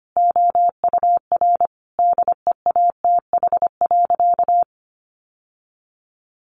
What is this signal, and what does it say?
Morse code, heard as a keyed tone, about 25 wpm, 700 Hz. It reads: OUR DEATH.
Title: Sentences from Top 1200 Words 25wpm